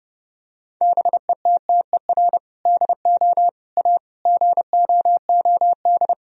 Morse code, heard as a keyed tone, about 32 wpm, 700 Hz. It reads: BETTER DO A GOOD
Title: Sentences from Top 1000 Words - Review Part 3 - 30wpm